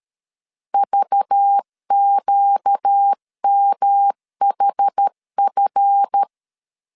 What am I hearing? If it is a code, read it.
Morse code, heard as a keyed tone, about 13 wpm, 800 Hz. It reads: VQMHF
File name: LEZIONI / 7 GF9Z / 7 GF9Z 12wpm 2147